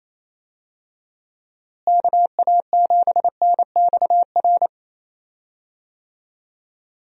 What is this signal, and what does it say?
Morse code, heard as a keyed tone, about 28 wpm, 700 Hz. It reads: KA7NXR
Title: USA Call Signs - 6 Characters - Repeated 2x - 28wpm